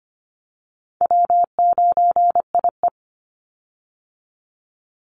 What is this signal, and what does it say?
Morse code, heard as a keyed tone, about 24 wpm, 700 Hz. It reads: W9IE
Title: Call Signs Repeated 2x - 25wpm